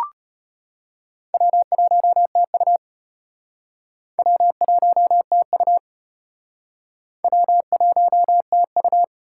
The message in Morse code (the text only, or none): W1TU W1TU W1TU